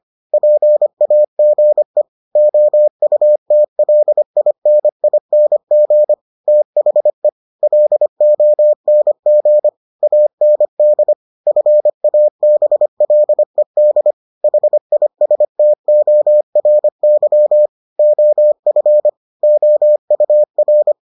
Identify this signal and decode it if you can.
Morse code, heard as a keyed tone, about 25 wpm, 600 Hz. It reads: PAGE OUTLINING THE LONG AND FABLED HISTORY OF OUR